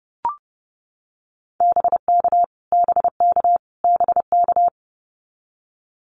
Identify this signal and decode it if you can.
Morse code, heard as a keyed tone, about 29 wpm, 700 Hz. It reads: BK BK BK
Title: Abbreviations Repeated 3x - 30wpm